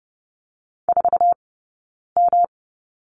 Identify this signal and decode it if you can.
Morse code, heard as a keyed tone, about 30 wpm, spaced as for 9 wpm, 690 Hz. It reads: 4M
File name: Lesson 18d - Letter m - Two Characters 30wpm